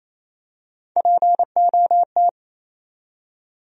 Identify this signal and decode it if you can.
Morse code, heard as a keyed tone, about 28 wpm, 700 Hz. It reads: POT
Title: USA Call Sign Postfixes - 3 Letters 28wpm